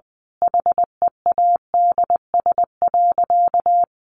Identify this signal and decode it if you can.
Morse code, heard as a keyed tone, about 20 wpm, 700 Hz. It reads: HEADS.
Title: A Sound Of Thunder 20wpm